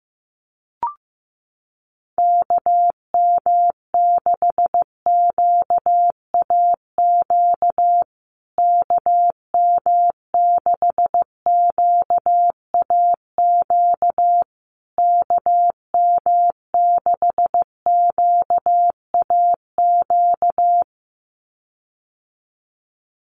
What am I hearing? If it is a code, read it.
Morse code, heard as a keyed tone, about 15 wpm, 700 Hz. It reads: KM6QAQ KM6QAQ KM6QAQ
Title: USA Call Signs Repeated 3x 15wpm